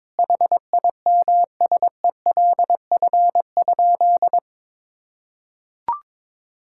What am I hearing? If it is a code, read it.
Morse code, heard as a keyed tone, about 22 wpm, 700 Hz. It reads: HIMSELF?
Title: Sentences from Top 1600 Words 22wpm